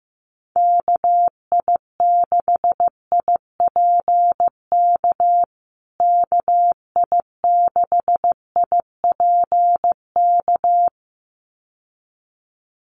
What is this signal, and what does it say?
Morse code, heard as a keyed tone, about 15 wpm, 700 Hz. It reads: KI6IPK KI6IPK
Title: USA Call Signs Repeated 2x 15wpm